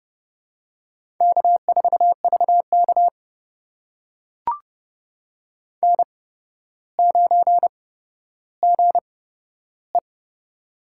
Morse code, heard as a keyed tone, about 31 wpm, spaced as for 13 wpm, 700 Hz. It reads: K4VK N9GE
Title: USA Call Signs - 4 Characters - 8x Character Spacing - 30wpm